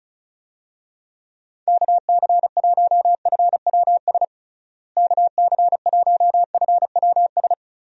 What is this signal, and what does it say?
Morse code, heard as a keyed tone, about 36 wpm, 700 Hz. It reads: KC1FWS KC1FWS
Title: USA Call Signs - 6 Characters - Repeated 2x - 35wpm